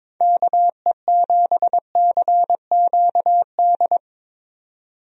Morse code, heard as a keyed tone, about 22 wpm, 700 Hz. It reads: KE7CQD
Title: USA Call Signs - 6 Characters - 22wpm